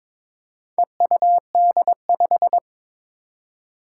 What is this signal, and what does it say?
Morse code, heard as a keyed tone, about 22 wpm, 700 Hz. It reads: EUD5
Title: Four Challenging Letter-Number Combinations 22wpm